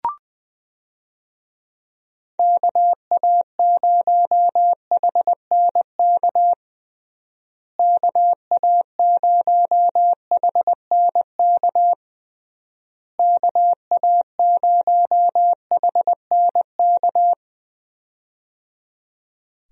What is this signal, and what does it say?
Morse code, heard as a keyed tone, about 20 wpm, 700 Hz. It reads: KA0HNK KA0HNK KA0HNK
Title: USA Call Signs - 6 Characters - Repeated 3x - 20wpm